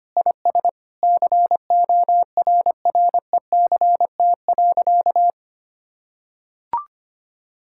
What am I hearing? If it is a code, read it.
Morse code, heard as a keyed tone, about 25 wpm, 700 Hz. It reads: IS CORRECT.
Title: Sentences from Top 1000 Words - Review Part 5 - 25wpm